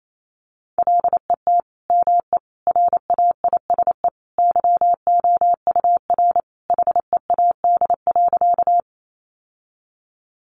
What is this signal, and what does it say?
Morse code, heard as a keyed tone, about 28 wpm, 700 Hz. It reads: LET ME RAISE YOUR HEAD.
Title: Sentences from Top 1400 Words 28wpm